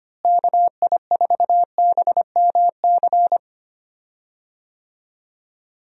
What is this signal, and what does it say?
Morse code, heard as a keyed tone, about 25 wpm, 700 Hz. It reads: KI4BMC
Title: USA Call Signs - 6 Characters - Repeated 3x - 25wpm